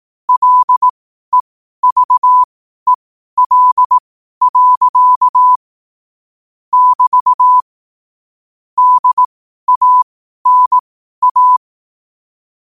Morse code, heard as a keyed tone, about 18 wpm, spaced as for 12 wpm, 1000 Hz. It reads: LEVEL. <BT> DANA
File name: qotdfiles / 12wpm